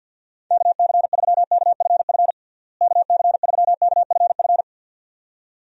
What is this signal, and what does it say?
Morse code, heard as a keyed tone, about 51 wpm, 700 Hz. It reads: KC3KFF KC3KFF
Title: USA Call Signs - 6 Characters - Repeated 2x - 50wpm